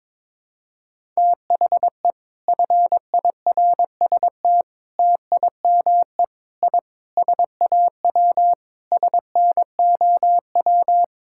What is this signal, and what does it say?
Morse code, heard as a keyed tone, about 21 wpm, 700 Hz. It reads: THE FIRST TIME I SAW SNOW
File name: practice / 22wpm / Sentences with 4-Word Phrases - 22wpm